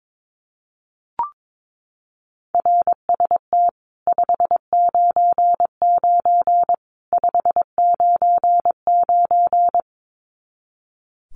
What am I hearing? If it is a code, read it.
Morse code, heard as a keyed tone, about 22 wpm, 700 Hz. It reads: RST 599 599
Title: Common QSO Phrases 22wpm